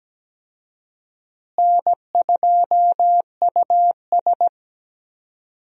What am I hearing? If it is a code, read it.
Morse code, heard as a keyed tone, about 17 wpm, 700 Hz. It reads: N2US
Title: USA Call Signs - 4 Characters - Repeated 3x - 17wpm